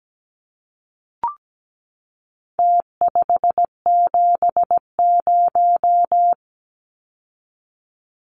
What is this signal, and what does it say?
Morse code, heard as a keyed tone, about 17 wpm, 700 Hz. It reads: T570
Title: Numbers - Contest Style - 4-Digit 17wpm